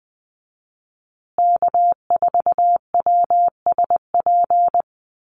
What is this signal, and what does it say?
Morse code, heard as a keyed tone, about 20 wpm, 700 Hz. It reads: K4WSP
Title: USA Call Signs Repeated 2x 20wpm